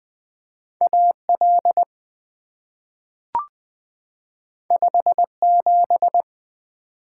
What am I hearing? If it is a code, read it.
Morse code, heard as a keyed tone, about 20 wpm, 700 Hz. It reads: A L 5 7
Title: Two Letter-Number Combinations - 20wpm